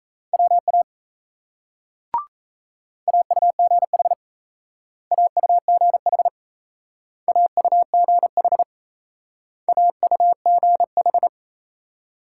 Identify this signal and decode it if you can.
Morse code, heard as a keyed tone, about 37 wpm, 700 Hz. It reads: WA AUGH AUGH AUGH AUGH